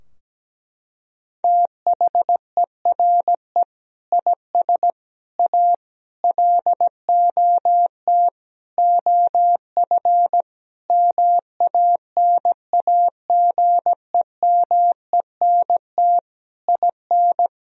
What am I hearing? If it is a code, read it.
Morse code, heard as a keyed tone, about 17 wpm, 700 Hz. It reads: THERE IS A LOT OF MANAGEMENT IN
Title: Sentences from Top 1000 Words - Review Part 4 - 17wpm